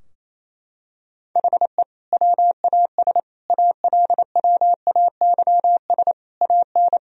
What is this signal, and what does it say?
Morse code, heard as a keyed tone, about 28 wpm, 700 Hz. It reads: HE WAS ALWAYS AN
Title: Sentences from Top 1000 Words - Review Part 2 - 28wpm